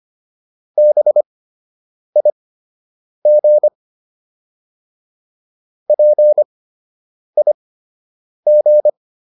Morse code, heard as a keyed tone, about 25 wpm, spaced as for 8 wpm, 600 Hz. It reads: BIG PIG